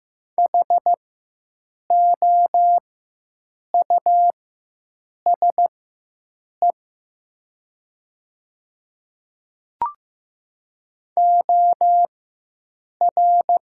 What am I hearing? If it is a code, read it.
Morse code, heard as a keyed tone, about 15 wpm, spaced as for 7 wpm, 700 Hz. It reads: HOUSE OR
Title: 4x Character Spacing - Top 300 Words - 15wpm